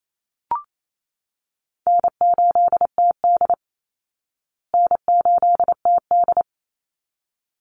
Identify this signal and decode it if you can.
Morse code, heard as a keyed tone, about 28 wpm, 690 Hz. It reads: N8TD N8TD
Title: Call Signs Repeated 3x - 28wpm